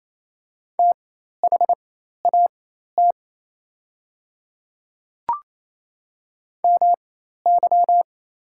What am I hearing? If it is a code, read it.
Morse code, heard as a keyed tone, about 28 wpm, spaced as for 13 wpm, 700 Hz. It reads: THAT MY
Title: 4x Character Spacing - Top 100 Words - 28wpm